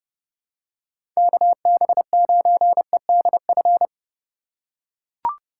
Morse code, heard as a keyed tone, about 30 wpm, 700 Hz. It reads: KB9EDF
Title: USA Call Signs - 4x Character Spacing - 30wpm